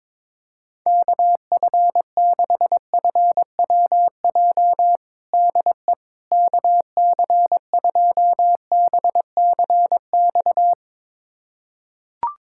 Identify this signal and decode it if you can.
Morse code, heard as a keyed tone, about 22 wpm, 700 Hz. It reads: KF6FWJ DE KC2BCX
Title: USA Callsign DE Callsign - 6 Characters - 22wpm